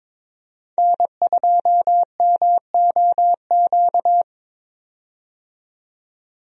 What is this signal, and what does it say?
Morse code, heard as a keyed tone, about 22 wpm, 700 Hz. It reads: N2MOQ
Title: USA Call Signs - 5 Characters - 22wpm